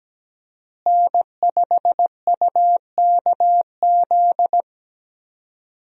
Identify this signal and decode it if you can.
Morse code, heard as a keyed tone, about 17 wpm, 700 Hz. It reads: N5UKZ
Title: USA Call Signs - 5 Characters - Repeated 2x - 17wpm